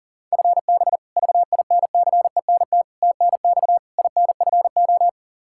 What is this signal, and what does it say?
Morse code, heard as a keyed tone, about 37 wpm, 700 Hz. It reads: FB VINCENT TNX INFO
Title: Practice QSO - Exchange 3 - More Info - 40wpm